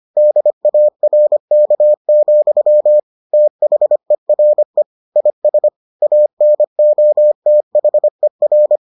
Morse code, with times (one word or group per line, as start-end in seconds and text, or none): 0.2-3.0 DARK,
3.3-4.8 THERE
5.2-5.7 IS
6.0-8.8 ANOTHER